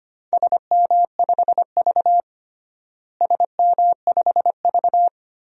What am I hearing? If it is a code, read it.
Morse code, heard as a keyed tone, about 25 wpm, 700 Hz. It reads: SM5V SM5V
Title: DXCC-100 Call Signs - 4 Characters - Repeated 2x - 25wpm